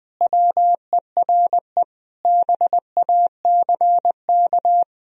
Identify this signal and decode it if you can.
Morse code, heard as a keyed tone, about 20 wpm, 700 Hz. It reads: WERE BACK
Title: Sentences from Top 1000 Words - Review Part 2 - 20wpm